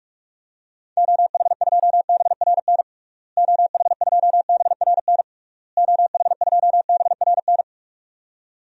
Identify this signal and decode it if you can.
Morse code, heard as a keyed tone, about 46 wpm, 700 Hz. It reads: OH2BRN OH2BRN OH2BRN
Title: DXCC-100 Call Signs - 6 Characters - Repeated 3x - 45wpm